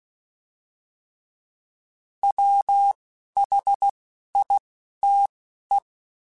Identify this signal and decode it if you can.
Morse code, heard as a keyed tone, about 16 wpm, spaced as for 11 wpm, 800 Hz. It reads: WHITE